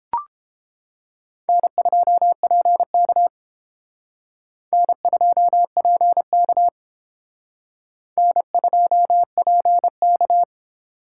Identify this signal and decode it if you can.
Morse code, heard as a keyed tone, about 29 wpm, 700 Hz. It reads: N2PK N2PK N2PK